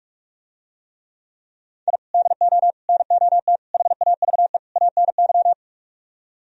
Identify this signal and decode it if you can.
Morse code, heard as a keyed tone, about 48 wpm, 700 Hz. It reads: I DO NOT HAVE ANY